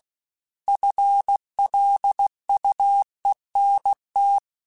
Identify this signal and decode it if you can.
Morse code, heard as a keyed tone, about 16 wpm, 800 Hz. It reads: FLUENT
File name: FL Runs / FL Runs mp3s / FL Runs @15